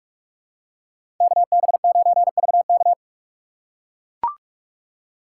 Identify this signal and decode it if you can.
Morse code, heard as a keyed tone, about 43 wpm, 700 Hz. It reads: KB9VK
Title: USA Call Signs - 4x Character Spacing - 45wpm